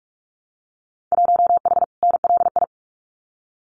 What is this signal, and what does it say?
Morse code, heard as a keyed tone, about 46 wpm, 700 Hz. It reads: 1H NLI
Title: Winter Field Day Exchanges 45wpm